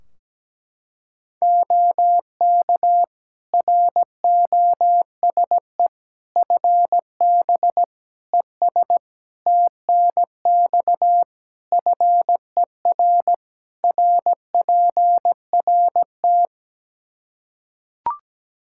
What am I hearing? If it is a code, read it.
Morse code, heard as a keyed tone, about 17 wpm, 700 Hz. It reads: OK ROSE FB ES TNX FER RPRT
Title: Practice QSO - Exchange 2 - Info - 17wpm